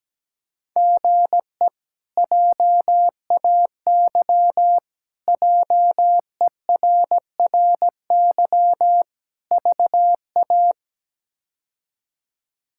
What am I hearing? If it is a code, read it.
Morse code, heard as a keyed tone, about 17 wpm, 700 Hz. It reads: GE JAY JERRY VA
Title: K1USN SST QSO Practice - 17wpm